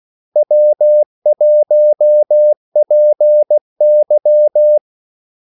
Morse code, heard as a keyed tone, about 16 wpm, 600 Hz. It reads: W1PY
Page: Kaldesignaler 16 wpm